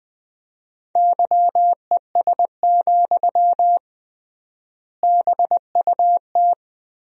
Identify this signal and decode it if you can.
Morse code, heard as a keyed tone, about 21 wpm, 700 Hz. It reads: YES, BUT
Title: 3x Word-Spacing - Sentences from Top 500 Words - 20wpm